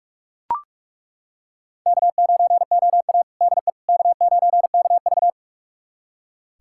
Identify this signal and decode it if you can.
Morse code, heard as a keyed tone, about 41 wpm, 700 Hz. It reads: K9OA DE K9KV
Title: USA Callsign DE Callsign - 4 Characters - 45wpm